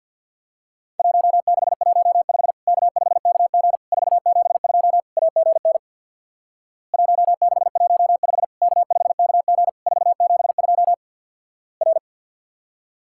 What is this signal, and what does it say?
Morse code, heard as a keyed tone, about 50 wpm, 700 Hz. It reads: 1615 K5KG 472 AGN 1615 K5KG 472 R
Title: Worked All Europe DX Contest - QTC Exchange with Repeats - 50wpm